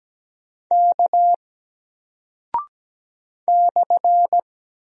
Morse code, heard as a keyed tone, about 17 wpm, 700 Hz. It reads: K /
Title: Single Character (Letter:Number:Punctuation:Prosign) 17wpm